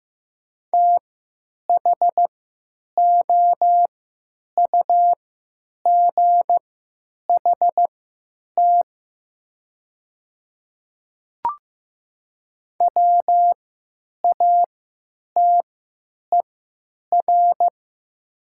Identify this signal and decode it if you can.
Morse code, heard as a keyed tone, about 15 wpm, spaced as for 9 wpm, 700 Hz. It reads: THOUGHT WATER
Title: 3x Character Spacing - Top 300 Words - 15wpm